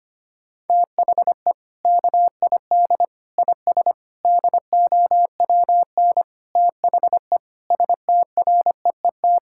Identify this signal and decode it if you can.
Morse code, heard as a keyed tone, about 25 wpm, 700 Hz. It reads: THE KID IS DOWN THE STREET
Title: Sentences from Top 1600 Words 25wpm